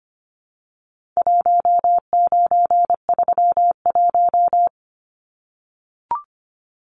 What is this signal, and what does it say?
Morse code, heard as a keyed tone, about 25 wpm, 700 Hz. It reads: 1931
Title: Numbers - 4-Digit 25wpm